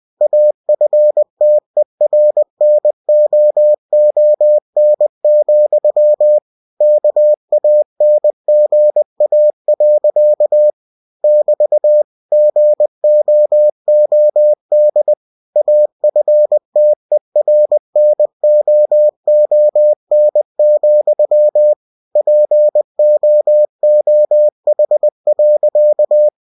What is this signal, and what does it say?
Morse code, heard as a keyed tone, about 20 wpm, 600 Hz. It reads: AFTERNOON, KANGA. <BT> GOOD AFTERNOON, POOH.
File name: cw-books / winnie-the-pooh / winnie-20wpm-page-0066